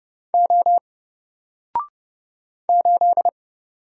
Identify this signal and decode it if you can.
Morse code, heard as a keyed tone, about 31 wpm, 700 Hz. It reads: O 8
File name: Single-Letter-Number-30wpm